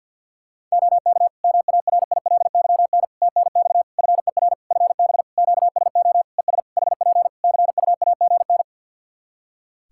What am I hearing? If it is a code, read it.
Morse code, heard as a keyed tone, about 52 wpm, 700 Hz. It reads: OK MARILYN TNX FER FB QSO ES HP CUAGN
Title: Practice QSO - Exchange 4 - Ending - 50wpm